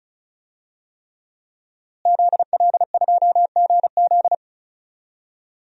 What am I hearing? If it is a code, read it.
Morse code, heard as a keyed tone, about 36 wpm, 700 Hz. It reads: ZL2GZ
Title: DXCC-100 Call Signs - 5 Characters - Repeated 3x - 35wpm